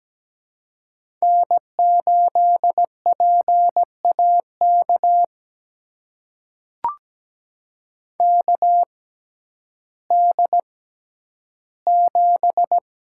Morse code, heard as a keyed tone, about 17 wpm, spaced as for 10 wpm, 700 Hz. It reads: N8PAK KD7
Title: USA Call Signs - 5 Characters - 6x Character Spacing - 17wpm